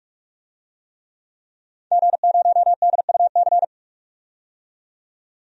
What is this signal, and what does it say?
Morse code, heard as a keyed tone, about 41 wpm, 700 Hz. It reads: G0DUC
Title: DXCC-100 Call Signs - 5 Characters - 45wpm